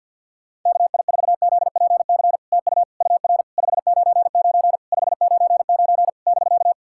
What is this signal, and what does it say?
Morse code, heard as a keyed tone, about 50 wpm, 660 Hz. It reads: KI4ZPC TU UR 599 599 <BK>
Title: SOTA QSO Practice 50wpm